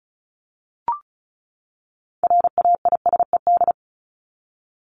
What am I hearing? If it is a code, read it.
Morse code, heard as a keyed tone, about 34 wpm, 700 Hz. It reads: RAISED
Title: ED Suffix - Top 100 Words with Base Words 35wpm